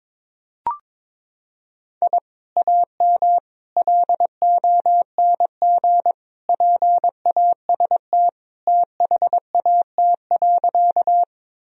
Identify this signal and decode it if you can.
Morse code, heard as a keyed tone, about 22 wpm, 700 Hz. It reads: I AM LONG PAST THAT.
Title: Sentences from Top 1000 Words - Review Part 3 - 22wpm